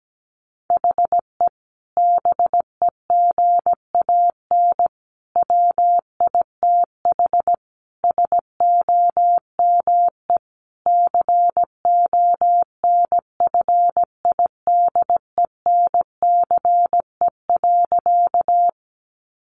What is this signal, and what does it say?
Morse code, heard as a keyed tone, about 17 wpm, 700 Hz. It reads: HE BEGAN WITH SOME CONFIDENCE.
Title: Sentences from Top 1500 Words 17wpm